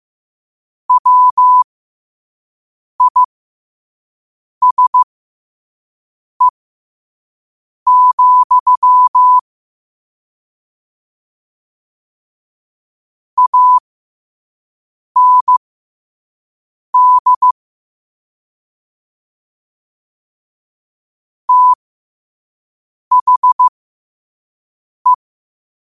Morse code, heard as a keyed tone, about 15 wpm, spaced as for 5 wpm, 1000 Hz. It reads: WISE, AND THE
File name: qotdfiles / 5wpm